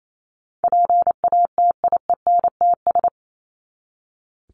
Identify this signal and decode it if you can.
Morse code, heard as a keyed tone, about 28 wpm, 700 Hz. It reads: PATIENTS
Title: ES:S Suffix - Top 100 Words with Base Words 28wpm